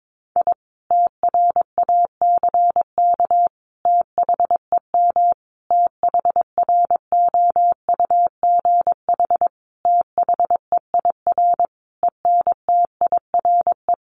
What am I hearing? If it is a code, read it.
Morse code, heard as a keyed tone, about 22 wpm, 700 Hz. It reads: I TRACK THEM THROUGH THEIR ENTIRE
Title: A Sound Of Thunder 22wpm